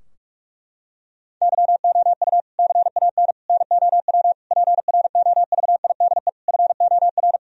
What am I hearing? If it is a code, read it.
Morse code, heard as a keyed tone, about 42 wpm, 700 Hz. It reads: YOU CAN NOW PROVIDE FOR
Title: Sentences from Top 1000 Words - Review Part 3 - 45wpm